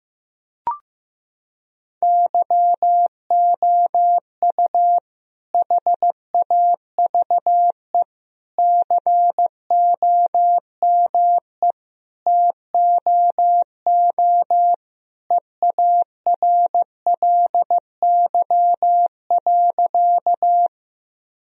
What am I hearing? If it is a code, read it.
Morse code, heard as a keyed tone, about 15 wpm, 700 Hz. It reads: YOU HAVE COME TOO EARLY.
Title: Sentences from Top 1000 Words - Review Part 2 - 15wpm